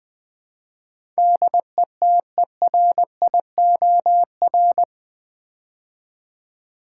Morse code, heard as a keyed tone, about 20 wpm, 700 Hz. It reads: DETERIOR
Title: Build A Word - 4 Letters At A Time - 11-Letter Words - 20wpm